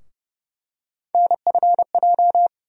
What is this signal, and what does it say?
Morse code, heard as a keyed tone, about 30 wpm, 700 Hz. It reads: NFJ
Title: USA Call Sign Postfixes - 3 Letters 30wpm